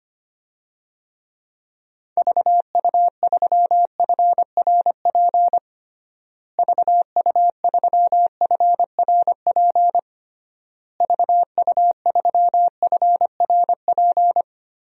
Morse code, heard as a keyed tone, about 25 wpm, 700 Hz. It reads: VU3FRP VU3FRP VU3FRP
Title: DXCC-100 Call Signs - 6 Characters - Repeated 3x - 25wpm